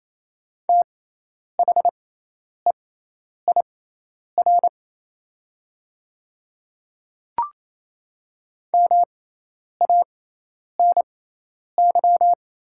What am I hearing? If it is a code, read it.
Morse code, heard as a keyed tone, about 28 wpm, spaced as for 10 wpm, 700 Hz. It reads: THEIR MANY
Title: 6x Character Spacing - Top 100 Words - 28wpm